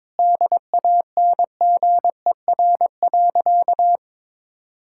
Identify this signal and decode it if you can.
Morse code, heard as a keyed tone, about 22 wpm, 700 Hz. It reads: DANGER.
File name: practice / 22wpm / Sentences with 2-Word Phrases - 22wpm